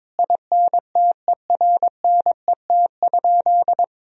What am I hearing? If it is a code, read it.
Morse code, heard as a keyed tone, about 22 wpm, 700 Hz. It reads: INTERNET?
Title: Sentences from Top 1000 Words - Review Part 4 - 22wpm